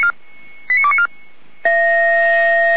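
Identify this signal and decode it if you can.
Five-tone selective call (ZVEI), heard as quick runs of five tones, tones of about 70 ms. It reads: 89294
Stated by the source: Five-tone ZVEI selective call 89123, received on an SDR by fire brigade Feuerwehr Orlamünde.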